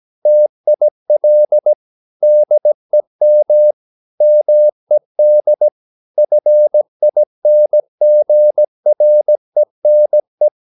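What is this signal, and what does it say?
Morse code, heard as a keyed tone, about 17 wpm, 600 Hz. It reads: TIL DEM MED FINGRENE
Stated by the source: Slow (QRS) Morse practice at 17 wpm, from CW med Gnister.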